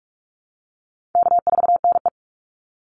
Morse code, heard as a keyed tone, about 44 wpm, 700 Hz. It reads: K4NE
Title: Call Signs Repeated 3x - 45wpm